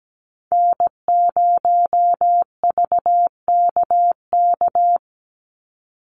Morse code, heard as a keyed tone, about 17 wpm, 700 Hz. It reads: N0VKK
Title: USA Call Signs 17wpm